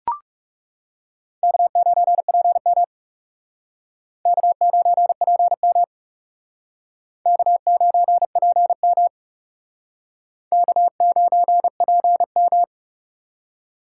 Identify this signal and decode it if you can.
Morse code, heard as a keyed tone, about 39 wpm, 700 Hz. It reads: K9PM K9PM K9PM K9PM